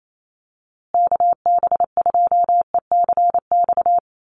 Morse code, heard as a keyed tone, about 28 wpm, 700 Hz. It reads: KB2ECX
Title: USA Call Signs Repeated 3x 28wpm